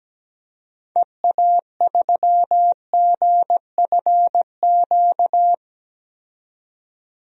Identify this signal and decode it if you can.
Morse code, heard as a keyed tone, about 17 wpm, 700 Hz. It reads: EA3GFQ
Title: DXCC-100 Call Signs - 6 Characters - 17wpm